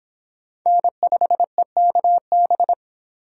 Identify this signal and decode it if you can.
Morse code, heard as a keyed tone, about 25 wpm, 700 Hz. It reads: N5EKB